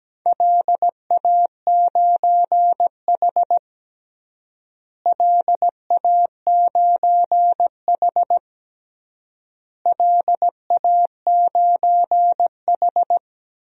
Morse code, heard as a keyed tone, about 17 wpm, 700 Hz. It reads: LA9H LA9H LA9H
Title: DXCC-100 Call Signs - 4 Characters - Repeated 3x - 17wpm